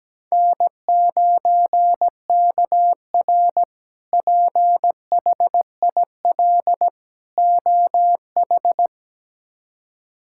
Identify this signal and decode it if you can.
Morse code, heard as a keyed tone, about 17 wpm, 700 Hz. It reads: N9KR PHIL OH
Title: K1USN SST QSO Practice - 17wpm